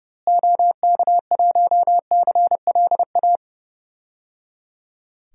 Morse code, heard as a keyed tone, about 29 wpm, 700 Hz. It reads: OK1CLA
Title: DXCC-100 Call Signs - 6 Characters - Repeated 3x - 30wpm